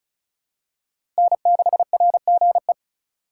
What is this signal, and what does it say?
Morse code, heard as a keyed tone, about 36 wpm, 700 Hz. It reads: N6RGE
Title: USA Call Signs - 4x Character Spacing - 35wpm